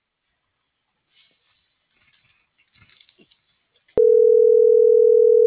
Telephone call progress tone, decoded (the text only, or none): ringback tone